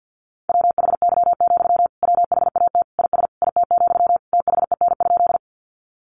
Morse code, heard as a keyed tone, about 50 wpm, 700 Hz. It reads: WHY, WHAT IS IT, THEN?
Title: Sentences from Top 200 Words 50wpm